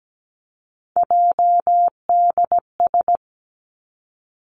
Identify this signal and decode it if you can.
Morse code, heard as a keyed tone, about 17 wpm, 700 Hz. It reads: JDS
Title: Three Challenging Letter-Number Combinations 17wpm